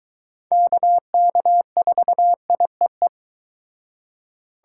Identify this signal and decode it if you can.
Morse code, heard as a keyed tone, about 22 wpm, 700 Hz. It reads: KK4IEE